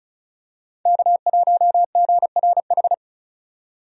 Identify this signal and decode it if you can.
Morse code, heard as a keyed tone, about 34 wpm, 700 Hz. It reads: K1GRH
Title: USA Call Signs - 5 Characters - 35wpm